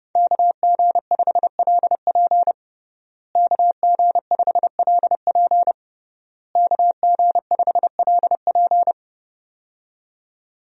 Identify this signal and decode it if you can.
Morse code, heard as a keyed tone, about 30 wpm, 700 Hz. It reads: KG5LP KG5LP KG5LP
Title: USA Call Signs - 5 Characters - Repeated 3x - 30wpm